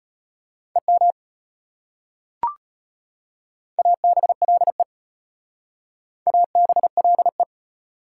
Morse code, heard as a keyed tone, about 34 wpm, 700 Hz. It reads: EM ABLE ABLE